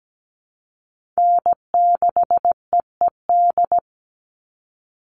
Morse code, heard as a keyed tone, about 17 wpm, 700 Hz. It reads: N6EED
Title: Call Signs - 17wpm